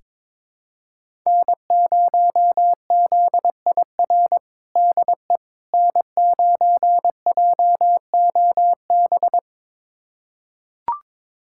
Morse code, heard as a keyed tone, about 22 wpm, 700 Hz. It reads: N0ZIR DE N9JOB
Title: USA Callsign DE Callsign - 5 Characters - 22wpm